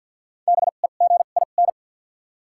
Morse code, heard as a keyed tone, about 52 wpm, spaced as for 36 wpm, 700 Hz. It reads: BEGIN
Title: 2x Character Spacing - Top 300 Words - 50wpm